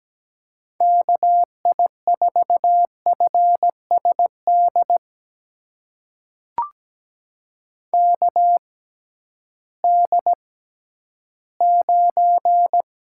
Morse code, heard as a keyed tone, about 17 wpm, spaced as for 11 wpm, 700 Hz. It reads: KI4FSD KD9
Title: USA Call Signs - 6 Characters - 6x Character Spacing - 17wpm